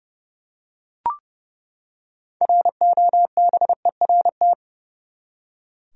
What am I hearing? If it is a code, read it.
Morse code, heard as a keyed tone, about 30 wpm, 700 Hz. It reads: ROBERT
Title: USA Common Names - 30wpm